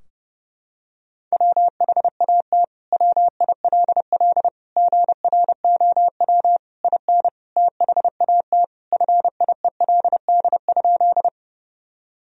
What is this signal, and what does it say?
Morse code, heard as a keyed tone, about 32 wpm, 700 Hz. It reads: WHAT WILL GROW IN THAT FIELD?
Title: Sentences from Top 1000 Words - Review Part 2 - 30wpm